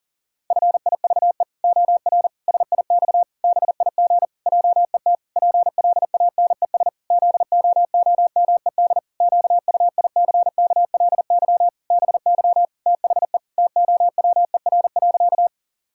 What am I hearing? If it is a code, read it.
Morse code, heard as a keyed tone, about 40 wpm, 700 Hz. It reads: FIVE OR SIX BIG JET PLANES ZOOMED QUICKLY BY THE TOWER.
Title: Pangram Challenge v2 - 40wpm